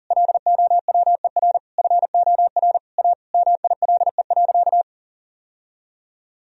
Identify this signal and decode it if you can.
Morse code, heard as a keyed tone, about 37 wpm, 700 Hz. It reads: LOWER FOR A MILE.
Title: Sentences from Top 1000 Words - Review Part 4 - 40wpm